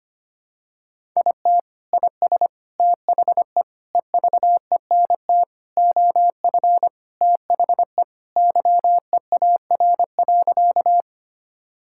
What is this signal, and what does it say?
Morse code, heard as a keyed tone, about 25 wpm, 700 Hz. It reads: IT IS THE EVENT OF THE YEAR.
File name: practice / 25wpm / Sentences with 3-Word Phrases - 25wpm